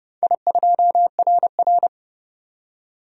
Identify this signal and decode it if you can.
Morse code, heard as a keyed tone, about 29 wpm, 700 Hz. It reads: I2RR
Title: DXCC-100 Call Signs - 4 Characters - 30wpm